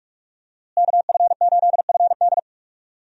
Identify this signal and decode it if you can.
Morse code, heard as a keyed tone, about 46 wpm, 700 Hz. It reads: KF8FD
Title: USA Call Signs - 5 Characters - Repeated 3x - 45wpm